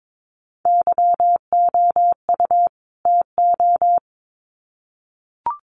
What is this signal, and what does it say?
Morse code, heard as a keyed tone, about 22 wpm, 700 Hz. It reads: YOU TO
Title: Sets of 2 Words 22wpm